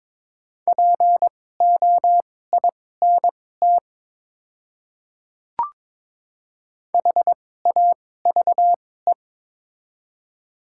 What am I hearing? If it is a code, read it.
Morse code, heard as a keyed tone, about 22 wpm, spaced as for 16 wpm, 700 Hz. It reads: POINT HAVE
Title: 2x Character Spacing - Top 300 Words - 22wpm